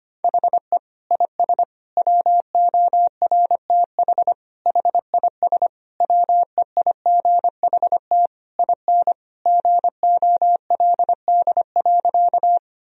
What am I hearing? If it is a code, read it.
Morse code, heard as a keyed tone, about 25 wpm, 700 Hz. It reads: HE IS WORTH HIS WEIGHT IN GOLD.
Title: Sentences from Top 1000 Words - Review Part 3 - 25wpm